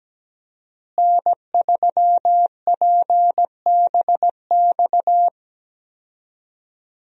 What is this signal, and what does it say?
Morse code, heard as a keyed tone, about 17 wpm, 700 Hz. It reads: N3PBX
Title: USA Call Signs - 5 Characters - Repeated 2x - 17wpm